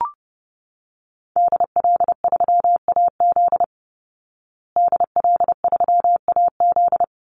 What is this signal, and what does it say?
Morse code, heard as a keyed tone, about 30 wpm, 700 Hz. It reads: DL3AZ DL3AZ
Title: Call Signs Repeated 3x - 30wpm